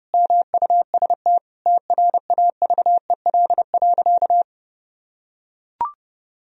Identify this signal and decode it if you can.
Morse code, heard as a keyed tone, about 30 wpm, 700 Hz. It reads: MUST TRAVEL.
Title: Sentences from Top 1000 Words - Review Part 2 - 30wpm